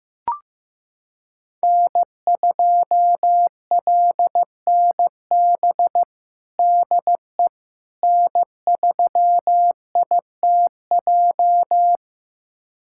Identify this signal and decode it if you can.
Morse code, heard as a keyed tone, about 16 wpm, 700 Hz. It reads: N2LNB DE N3ITJ
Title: USA Callsign DE Callsign - 5 Characters - 15wpm